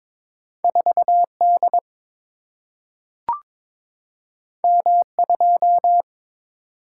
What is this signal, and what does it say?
Morse code, heard as a keyed tone, about 22 wpm, 700 Hz. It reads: 4 D M 2
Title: Single-Letter Single-Number 22wpm